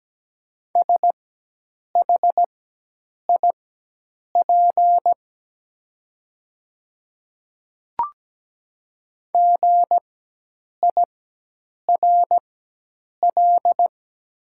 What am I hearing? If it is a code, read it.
Morse code, heard as a keyed tone, about 17 wpm, spaced as for 8 wpm, 700 Hz. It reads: SHIP GIRL
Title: 4x Character Spacing - Top 500 Words - 17wpm